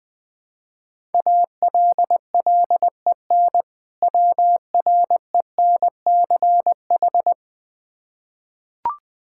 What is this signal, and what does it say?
Morse code, heard as a keyed tone, about 20 wpm, 700 Hz. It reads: ALLEN WRENCH
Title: Difficult Words with Hints - 20wpm